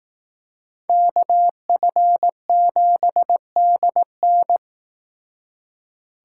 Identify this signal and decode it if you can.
Morse code, heard as a keyed tone, about 18 wpm, 700 Hz. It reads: KF7DN
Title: Speed Racing - USA 2x2 Call Signs 15wpm